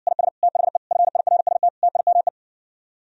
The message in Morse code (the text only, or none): IS THE FIRST TIME